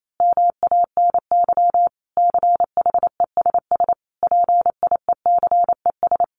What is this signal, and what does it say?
Morse code, heard as a keyed tone, about 28 wpm, 700 Hz. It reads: MANY CHESS PIECES
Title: A Sound Of Thunder 28wpm